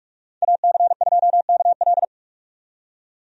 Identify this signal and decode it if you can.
Morse code, heard as a keyed tone, about 41 wpm, 700 Hz. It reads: AC2KL
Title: USA Call Signs - 5 Characters - 45wpm